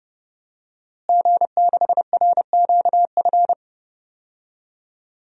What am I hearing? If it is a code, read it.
Morse code, heard as a keyed tone, about 30 wpm, 700 Hz. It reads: G6RQF
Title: DXCC-100 Call Signs - 5 Characters - Repeated 3x - 30wpm